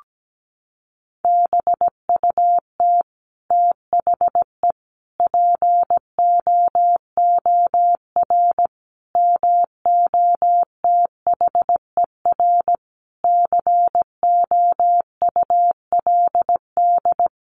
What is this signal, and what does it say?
Morse code, heard as a keyed tone, about 17 wpm, 700 Hz. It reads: BUT THE POOR MOTHER COULD
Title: Sentences from Top 1000 Words 17wpm